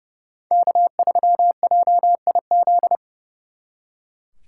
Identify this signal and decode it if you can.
Morse code, heard as a keyed tone, about 39 wpm, 700 Hz. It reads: K3JIZ